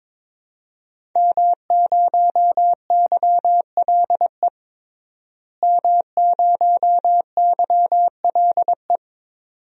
Morse code, heard as a keyed tone, about 22 wpm, 700 Hz. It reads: M0YLE M0YLE
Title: DXCC-100 Call Signs - 5 Characters - Repeated 2x - 22wpm